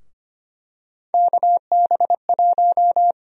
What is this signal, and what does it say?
Morse code, heard as a keyed tone, about 25 wpm, 700 Hz. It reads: KB1
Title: USA Call Sign Prefixes - Repeated 2x - 25wpm